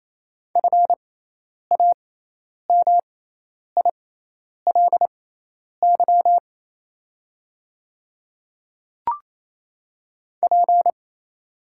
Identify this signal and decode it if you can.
Morse code, heard as a keyed tone, about 28 wpm, spaced as for 10 wpm, 700 Hz. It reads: FAMILY P
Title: 6x Character Spacing - Top 500 Words - 28wpm